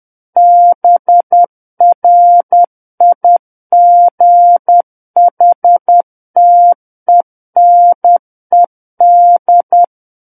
Morse code, heard as a keyed tone, about 10 wpm, 700 Hz. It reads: BRIGHTENED
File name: Book One / 01-Chapter 1 - 10 WPM